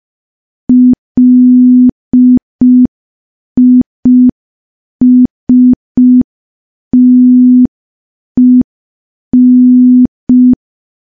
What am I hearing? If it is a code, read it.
Morse code, heard as a keyed tone, about 5 wpm, 250 Hz. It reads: LISTEN
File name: morse / morse-training-05WPM